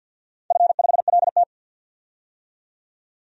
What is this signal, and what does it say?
Morse code, heard as a keyed tone, about 51 wpm, 700 Hz. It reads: F5LT
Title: DXCC-100 Call Signs - 4 Characters - 50wpm